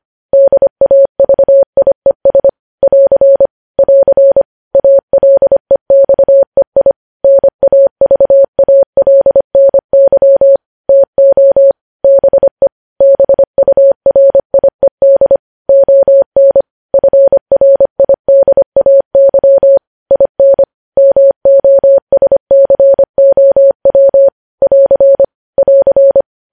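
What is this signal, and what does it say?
Morse code, heard as a keyed tone, about 25 wpm, 550 Hz. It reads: DAVIES <AR> <AR> ALEXEI NAVALNY TO BE BURIED ON FRIDAY IN MOSCOW <AR> <AR>
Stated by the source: Morse code news headlines sent at 25 wpm.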